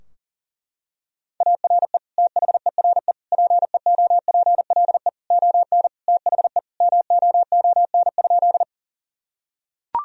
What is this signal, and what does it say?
Morse code, heard as a keyed tone, about 37 wpm, 700 Hz. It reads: ARE THERE PEOPLE ON THE MOON?
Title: Sentences from Top 1000 Words - Review Part 3 - 40wpm